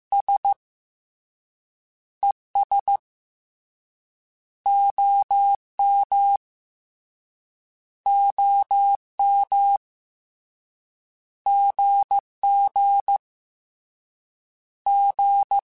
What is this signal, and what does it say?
Morse code, heard as a keyed tone, about 15 wpm, 2200 Hz. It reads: S E S O M O M G G G
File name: Abreviations / Abreviations mp3s / Abreviations@13